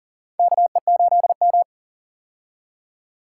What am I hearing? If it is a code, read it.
Morse code, heard as a keyed tone, about 40 wpm, 700 Hz. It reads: KE8M
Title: K1USN SST QSO Practice - 40wpm